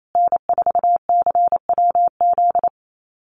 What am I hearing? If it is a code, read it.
Morse code, heard as a keyed tone, about 28 wpm, 700 Hz. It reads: N4CWZ
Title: USA Call Signs Repeated 3x 28wpm